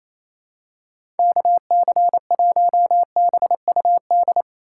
Morse code, heard as a keyed tone, about 28 wpm, 700 Hz. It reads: KC1BUD
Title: USA Call Signs - 6 Characters - 3x Character Spacing - 28wpm